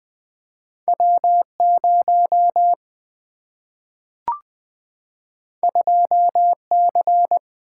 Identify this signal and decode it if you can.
Morse code, heard as a keyed tone, about 20 wpm, 700 Hz. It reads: W 0 2 C
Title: Single-Letter Single-Number 20wpm